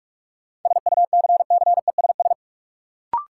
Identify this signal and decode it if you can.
Morse code, heard as a keyed tone, about 41 wpm, 700 Hz. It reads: SUCCESS
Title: Sentences from Top 900 Words - New Word Drill - 45wpm